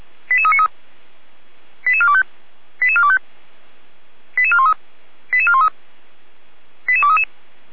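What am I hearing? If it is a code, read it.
Five-tone selective call (ZVEI), heard as quick runs of five tones, tones of about 70 ms. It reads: 88393 88425 88425 88413 88413 88233
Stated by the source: ZVEI five-tone call received on an SDR.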